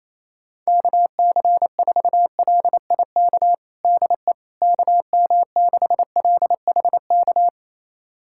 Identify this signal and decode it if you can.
Morse code, heard as a keyed tone, about 28 wpm, 700 Hz. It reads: KC4LIK DE KM6LHK
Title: USA Callsign DE Callsign - 6 Characters - 28wpm